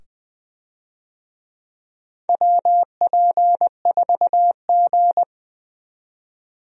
Morse code, heard as a keyed tone, about 20 wpm, 700 Hz. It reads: WP4G
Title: USA Call Signs - 4 Characters - 20wpm